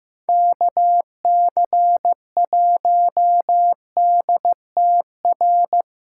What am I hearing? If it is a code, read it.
Morse code, heard as a keyed tone, about 15 wpm, 700 Hz. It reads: KC1DTR
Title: USA Callsign DE Callsign 15wpm